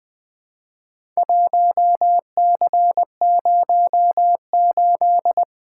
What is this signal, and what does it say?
Morse code, heard as a keyed tone, about 20 wpm, 700 Hz. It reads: 1C08
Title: Four Letter-Number Combinations - 20wpm